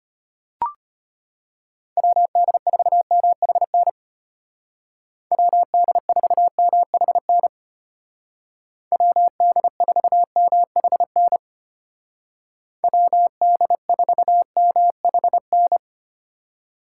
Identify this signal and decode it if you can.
Morse code, heard as a keyed tone, about 33 wpm, 700 Hz. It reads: WD4MHN WD4MHN WD4MHN WD4MHN